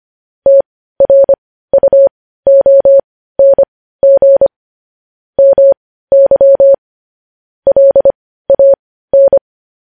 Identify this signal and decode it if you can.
Morse code, heard as a keyed tone, about 25 wpm, spaced as for 15 wpm, 550 Hz. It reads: TRUONG MY LAN